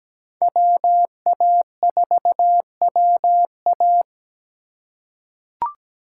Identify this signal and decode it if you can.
Morse code, heard as a keyed tone, about 17 wpm, 700 Hz. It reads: WA4WA
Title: USA Call Signs - 5 Characters - 3x Character Spacing - 17wpm